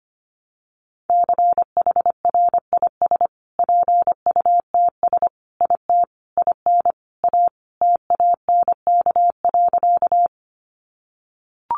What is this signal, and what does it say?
Morse code, heard as a keyed tone, about 25 wpm, 700 Hz. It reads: CHRIS PUTS IT IN A TANK.
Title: Additional Easy Phrases 25wpm